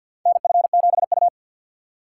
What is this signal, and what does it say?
Morse code, heard as a keyed tone, about 51 wpm, 700 Hz. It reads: NF7U
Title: USA Call Signs - 4 Characters - 3x Character Spacing - 50wpm